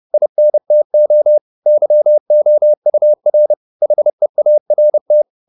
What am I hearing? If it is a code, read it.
Morse code, heard as a keyed tone, about 30 wpm, 600 Hz. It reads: INTO YOUR HEART
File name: the old man and the sea / the old man and the sea 1 30wpm